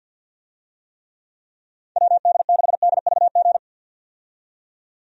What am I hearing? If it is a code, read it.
Morse code, heard as a keyed tone, about 46 wpm, 700 Hz. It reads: WD6DVG
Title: USA Call Signs - 6 Characters - 50wpm